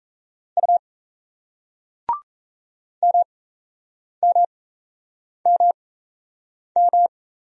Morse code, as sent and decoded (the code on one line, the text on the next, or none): ..- / -- -- -- --
U MMMM